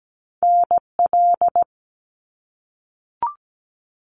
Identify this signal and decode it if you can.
Morse code, heard as a keyed tone, about 17 wpm, 700 Hz. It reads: NL
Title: Canadian Provinces and Territories - Repeated 3x - 17wpm